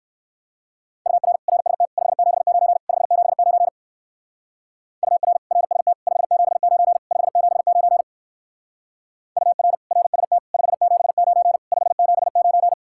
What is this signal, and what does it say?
Morse code, heard as a keyed tone, about 54 wpm, 700 Hz. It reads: UR RST 579 579 UR RST 579 579 UR RST 579 579
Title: Speed-Racing Common QSO Phrases - 45wpm